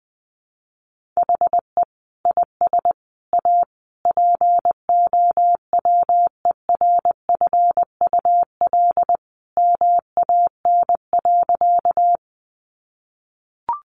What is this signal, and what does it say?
Morse code, heard as a keyed tone, about 20 wpm, 700 Hz. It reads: HE IS A POWERFUL MAN.
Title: Sentences from Top 1300 Words 20wpm